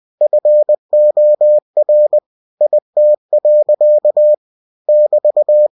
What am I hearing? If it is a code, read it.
Morse code, heard as a keyed tone, about 20 wpm, 600 Hz. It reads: FOR IT. <BT>
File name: cw-books / winnie-the-pooh / winnie-20wpm-page-0040